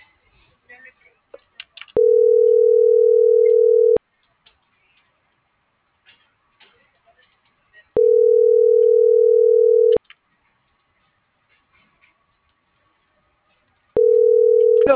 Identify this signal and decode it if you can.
Ringback tone (North America), 440 + 480 Hz.